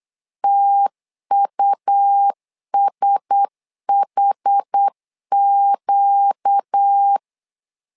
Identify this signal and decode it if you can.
Morse code, heard as a keyed tone, about 8 wpm, 800 Hz. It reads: TUSHQ
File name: LEZIONI / 5 LBVC / 5 L 8wpm 2831